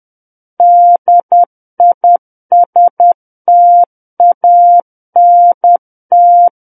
Morse code, heard as a keyed tone, about 10 wpm, 700 Hz. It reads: DISTANT
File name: Book One / 01-Chapter 1 - 10 WPM